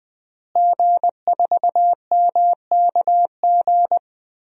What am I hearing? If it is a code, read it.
Morse code, heard as a keyed tone, about 20 wpm, 700 Hz. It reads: G4MKG
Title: DXCC-100 Call Signs - 5 Characters - Repeated 3x - 20wpm